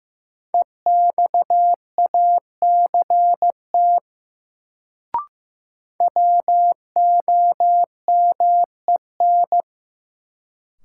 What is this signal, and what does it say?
Morse code, heard as a keyed tone, about 15 wpm, 700 Hz. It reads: EXACT WOMEN
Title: Sentences from Top 1300 Words - New Word Drill Alt - 15wpm